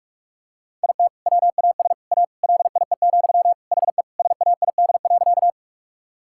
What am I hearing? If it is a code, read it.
Morse code, heard as a keyed tone, about 46 wpm, 700 Hz. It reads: IT WAS A LIE, HE SAID.
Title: Sentences from Top 1600 Words 45wpm